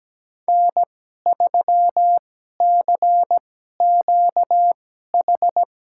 Morse code, heard as a keyed tone, about 17 wpm, spaced as for 12 wpm, 700 Hz. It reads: N3CQH
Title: USA Call Signs - 5 Characters - 2x Character Spacing - 17wpm